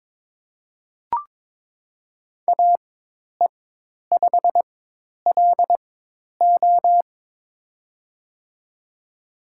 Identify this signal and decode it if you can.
Morse code, heard as a keyed tone, about 22 wpm, spaced as for 10 wpm, 700 Hz. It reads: AE5LO
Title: USA Call Signs - 5 Characters - 4x Character Spacing - 22wpm